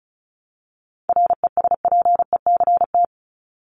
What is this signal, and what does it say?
Morse code, heard as a keyed tone, about 34 wpm, 700 Hz. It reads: RESPECT
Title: Top 1501-1750 Words 35wpm